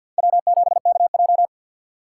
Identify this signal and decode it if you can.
Morse code, heard as a keyed tone, about 47 wpm, 700 Hz. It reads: W7KJ
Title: USA Call Signs - 4 Characters - 3x Character Spacing - 50wpm